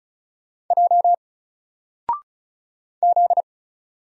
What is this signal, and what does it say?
Morse code, heard as a keyed tone, about 34 wpm, 700 Hz. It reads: J Z
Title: Single Character (Letter:Number:Punctuation:Prosign) 35wpm